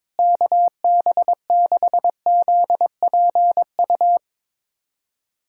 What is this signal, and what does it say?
Morse code, heard as a keyed tone, about 22 wpm, 700 Hz. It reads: KB6ZPU
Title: USA Callsign DE Callsign 22wpm